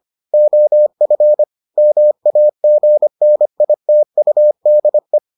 Morse code, heard as a keyed tone, about 25 wpm, 600 Hz. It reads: OF MAGNITUDE